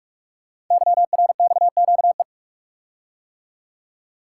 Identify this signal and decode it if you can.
Morse code, heard as a keyed tone, about 44 wpm, 700 Hz. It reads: YRXQE
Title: Code Groups - 5 Letters - 45wpm